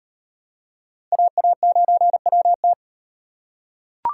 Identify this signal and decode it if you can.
Morse code, heard as a keyed tone, about 36 wpm, 700 Hz. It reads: AA9WT